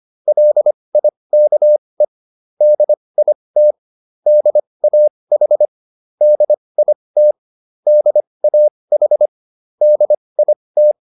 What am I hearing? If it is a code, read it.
Morse code, heard as a keyed tone, about 25 wpm, spaced as for 20 wpm, 600 Hz. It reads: LIKE DIT DAH DIT DAH DIT
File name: SS315 20